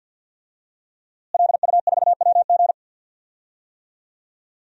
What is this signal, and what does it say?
Morse code, heard as a keyed tone, about 49 wpm, 700 Hz. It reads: LU4WG
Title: DXCC-100 Call Signs - 5 Characters - Repeated 3x - 50wpm